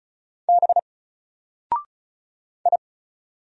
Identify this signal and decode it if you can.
Morse code, heard as a keyed tone, about 35 wpm, 700 Hz. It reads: B I
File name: Single-Letter-Number-35wpm